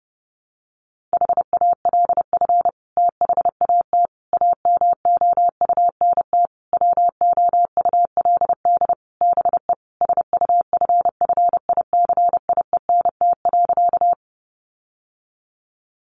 Morse code, heard as a keyed tone, about 30 wpm, 700 Hz. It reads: HALF THAT AMOUNT WOULD BE SUFFICIENT.
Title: Sentences from Top 1400 Words 30wpm